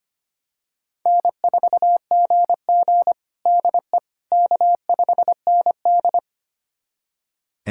Morse code, heard as a keyed tone, about 25 wpm, 700 Hz. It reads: N4GG DE K5ND
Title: USA Callsign DE Callsign - 4 Characters - 25wpm